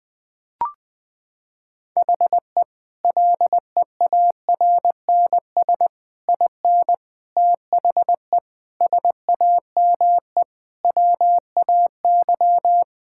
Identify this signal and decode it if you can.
Morse code, heard as a keyed tone, about 19 wpm, 700 Hz. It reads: HE LEARNS IN THE SAME WAY
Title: Sentences with 5-Word Phrases - 20wpm